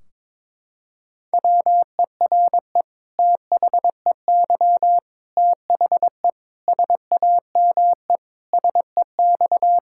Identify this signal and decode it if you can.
Morse code, heard as a keyed tone, about 22 wpm, 700 Hz. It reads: WERE THEY THE SAME SEX
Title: Sentences from Top 1600 Words 22wpm